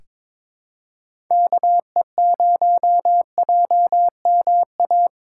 Morse code, heard as a keyed tone, about 22 wpm, 700 Hz. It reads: KE0JMA
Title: USA Call Signs - 6 Characters - 3x Character Spacing - 22wpm